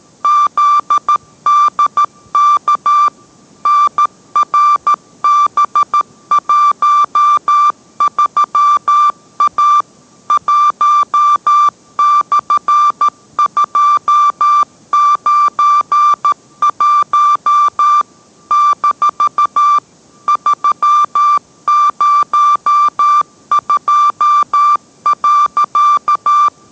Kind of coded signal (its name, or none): Morse code